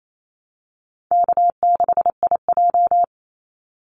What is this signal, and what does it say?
Morse code, heard as a keyed tone, about 28 wpm, 700 Hz. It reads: K6IJ
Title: Easy Call Signs Repeated 3x - 28wpm